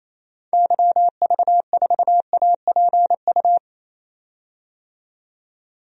Morse code, heard as a keyed tone, about 28 wpm, 700 Hz. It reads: YV4APU
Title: DXCC-100 Call Signs - 6 Characters - 28wpm